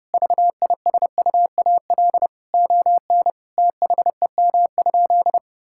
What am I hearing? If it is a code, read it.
Morse code, heard as a keyed tone, about 30 wpm, 700 Hz. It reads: VISUAL ON THEM?
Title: Sentences from Top 1000 Words - Review Part 6 - 30wpm